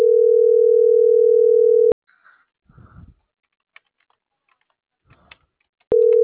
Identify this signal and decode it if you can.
Ringback tone (North America), 440 + 480 Hz.
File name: CENTRAL MAERSK / out gabrielle.melendez - .28623